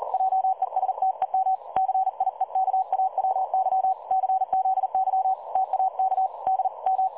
Morse code, heard as a keyed tone, about 39 wpm, 760 Hz. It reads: JESTEM CIEKAWY CZY TAK D M
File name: dlazaawansowanych / nagraniazpasma / websdr recording 3565.0kHz